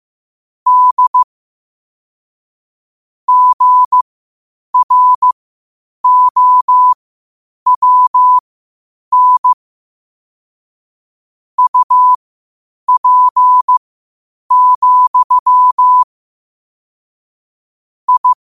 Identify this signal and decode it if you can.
Morse code, heard as a keyed tone, about 15 wpm, spaced as for 8 wpm, 1000 Hz. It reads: D GROWN UP, I
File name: qotdfiles / 8wpm